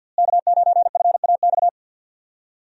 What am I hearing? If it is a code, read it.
Morse code, heard as a keyed tone, about 49 wpm, 700 Hz. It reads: K9FAX
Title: USA Call Signs - 5 Characters - Repeated 3x - 50wpm